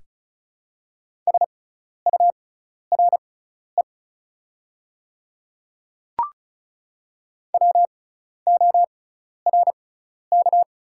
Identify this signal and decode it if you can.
Morse code, heard as a keyed tone, about 36 wpm, spaced as for 12 wpm, 700 Hz. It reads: SURE WORK
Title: 6x Character Spacing - Top 300 Words - 35wpm